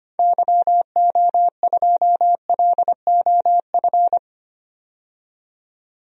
Morse code, heard as a keyed tone, about 25 wpm, 700 Hz. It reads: YO2LOF
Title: DXCC-100 Call Signs - 6 Characters - 25wpm